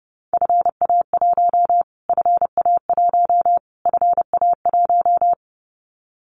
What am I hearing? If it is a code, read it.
Morse code, heard as a keyed tone, about 30 wpm, 700 Hz. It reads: FA1 FA1 FA1
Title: Callsign Prefixes for Top 25 DXCC - Repeated 3x - 30wpm